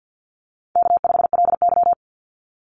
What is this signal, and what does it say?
Morse code, heard as a keyed tone, about 50 wpm, 700 Hz. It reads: K5LY
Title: Easy Call Signs Repeated 2x - 50wpm